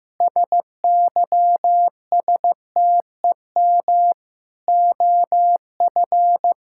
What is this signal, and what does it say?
Morse code, heard as a keyed tone, about 16 wpm, 700 Hz. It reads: SYSTEM OF
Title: Sentences from Top 1000 Words - Review Part 6 - 15wpm